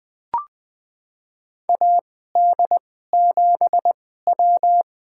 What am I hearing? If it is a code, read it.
Morse code, heard as a keyed tone, about 20 wpm, spaced as for 14 wpm, 700 Hz. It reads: AD7W
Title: USA Call Signs - 2x Character Spacing - 20wpm